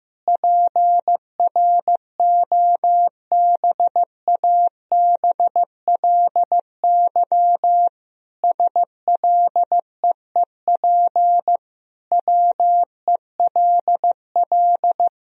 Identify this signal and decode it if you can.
Morse code, heard as a keyed tone, about 15 wpm, 700 Hz. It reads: PROBABLY SLEEP WELL
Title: Sentences from Top 1000 Words - Review Part 4 - 15wpm